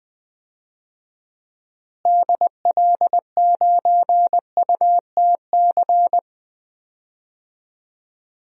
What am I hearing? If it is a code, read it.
Morse code, heard as a keyed tone, about 20 wpm, 700 Hz. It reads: DL9UTC
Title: DXCC-100 Call Signs - 6 Characters - Repeated 3x - 20wpm